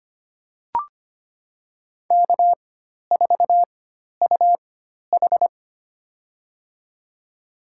Morse code, heard as a keyed tone, about 25 wpm, spaced as for 12 wpm, 700 Hz. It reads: K4UH
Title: USA Call Signs - 4 Characters - 4x Character Spacing - 25wpm